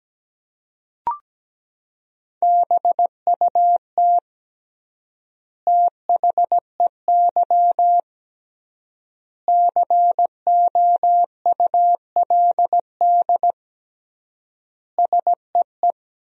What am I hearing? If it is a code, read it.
Morse code, heard as a keyed tone, about 17 wpm, 700 Hz. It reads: BUT THEY COULD SEE
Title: 3x Word-Spacing - Sentences from Top 500 Words - 17wpm